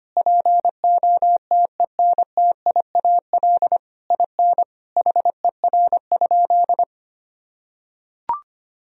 Morse code, heard as a keyed tone, about 25 wpm, 700 Hz. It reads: POTENTIAL IN HER?
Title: Sentences from Top 1000 Words - Review Part 4 - 25wpm